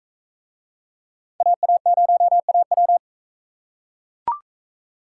Morse code, heard as a keyed tone, about 40 wpm, 700 Hz. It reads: AA0AW